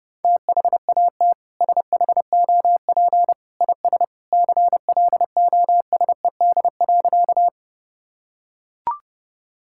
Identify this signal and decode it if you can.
Morse code, heard as a keyed tone, about 32 wpm, 700 Hz. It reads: THAT SHOP IS CLOSED.
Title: Sentences from Top 1600 Words 30wpm